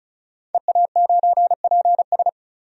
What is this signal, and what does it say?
Morse code, heard as a keyed tone, about 36 wpm, 700 Hz. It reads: EA9PS
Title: DXCC-100 Call Signs - 5 Characters - 35wpm